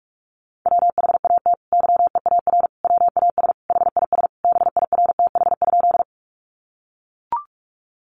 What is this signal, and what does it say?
Morse code, heard as a keyed tone, about 44 wpm, 700 Hz. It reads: WHAT YEAR WAS HIS BIRTH?
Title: Sentences from Top 800 Words 45wpm